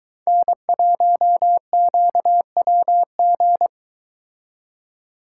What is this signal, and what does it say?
Morse code, heard as a keyed tone, about 22 wpm, 700 Hz. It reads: N1QWG